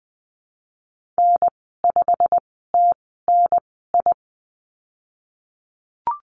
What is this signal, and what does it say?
Morse code, heard as a keyed tone, about 20 wpm, spaced as for 15 wpm, 700 Hz. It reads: N5TNI
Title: 147 Lesson 24g - Letter y - Call Signs 20wpm